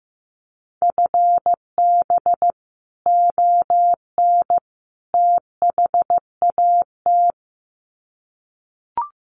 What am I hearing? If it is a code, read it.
Morse code, heard as a keyed tone, about 15 wpm, 700 Hz. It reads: FB ON THAT
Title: Common QSO Phrases 15wpm